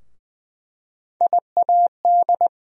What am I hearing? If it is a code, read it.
Morse code, heard as a keyed tone, about 20 wpm, 700 Hz. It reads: IAD
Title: USA Call Sign Postfixes - 3 Letters 20wpm